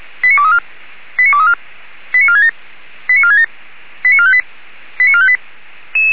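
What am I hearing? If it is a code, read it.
Five-tone selective call (ZVEI), heard as quick runs of five tones, tones of about 70 ms. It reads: 89234 89234 89467 89467 89458 89458
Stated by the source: ZVEI five-tone call 89375 received by SDR.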